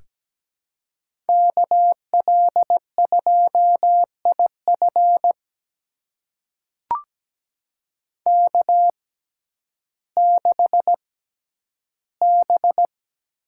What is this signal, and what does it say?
Morse code, heard as a keyed tone, about 17 wpm, spaced as for 10 wpm, 700 Hz. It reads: KL2IF K6B
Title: USA Call Signs - 5 Characters - 6x Character Spacing - 17wpm